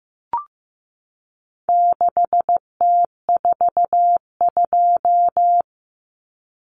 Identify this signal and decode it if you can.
Morse code, heard as a keyed tone, about 15 wpm, 700 Hz. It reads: 6T42
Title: Common Cut Numbers - 4-Digit 15wpm